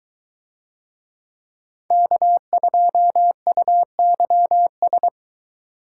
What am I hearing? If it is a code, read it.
Morse code, heard as a keyed tone, about 22 wpm, 700 Hz. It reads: K2UYS